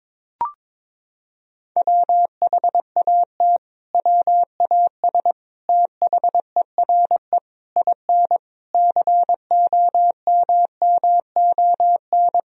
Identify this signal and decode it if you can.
Morse code, heard as a keyed tone, about 22 wpm, 700 Hz. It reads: WHAT WAS THERE IN COMMON
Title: Sentences from Top 1000 Words - Review Part 3 - 22wpm